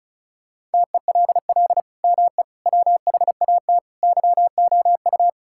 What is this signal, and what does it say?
Morse code, heard as a keyed tone, about 34 wpm, 700 Hz. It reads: TELL ME WHAT YOU
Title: Sentences from Top 1600 Words 35wpm